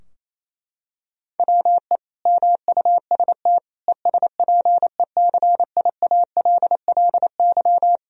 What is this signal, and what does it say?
Morse code, heard as a keyed tone, about 27 wpm, 700 Hz. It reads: WE MUST ESPECIALLY
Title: Sentences from Top 1000 Words - Review Part 4 - 28wpm